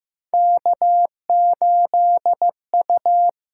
Morse code, heard as a keyed tone, about 15 wpm, 700 Hz. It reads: K8U
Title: USA Call Signs - 4 Characters - 6x Character Spacing - 15wpm